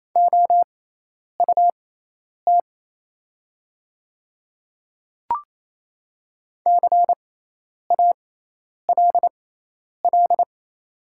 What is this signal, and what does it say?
Morse code, heard as a keyed tone, about 28 wpm, spaced as for 10 wpm, 700 Hz. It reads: OUT CALL
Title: 6x Character Spacing - Top 200 Words - 28wpm